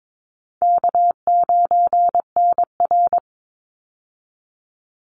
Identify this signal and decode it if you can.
Morse code, heard as a keyed tone, about 22 wpm, 700 Hz. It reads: K9NR
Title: Easy Call Signs - 22wpm